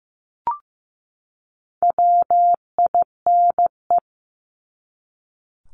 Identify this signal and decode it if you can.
Morse code, heard as a keyed tone, about 15 wpm, 700 Hz. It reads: WINE
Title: Top 1501-1750 Words 15wpm